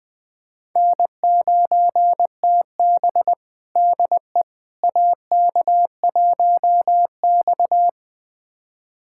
Morse code, heard as a keyed tone, about 21 wpm, 700 Hz. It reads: N9TB DE AK1X
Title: USA Callsign DE Callsign - 4 Characters - 20wpm